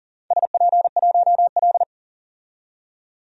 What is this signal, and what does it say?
Morse code, heard as a keyed tone, about 38 wpm, 700 Hz. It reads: SP1L
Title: DXCC-100 Call Signs - 4 Characters - 40wpm